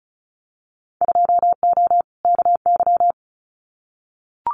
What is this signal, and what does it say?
Morse code, heard as a keyed tone, about 36 wpm, 700 Hz. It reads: 2O KY
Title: Winter Field Day Exchanges 35wpm